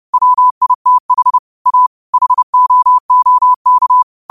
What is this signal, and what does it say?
Morse code, heard as a keyed tone, about 30 wpm, 1000 Hz. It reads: WITH A HOOK